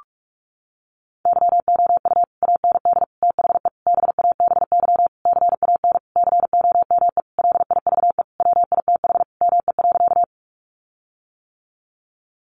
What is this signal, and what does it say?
Morse code, heard as a keyed tone, about 44 wpm, 700 Hz. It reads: YOU AND THE BABY CAN COME LIVE WITH ME.
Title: Sentences from Top 1300 Words 45wpm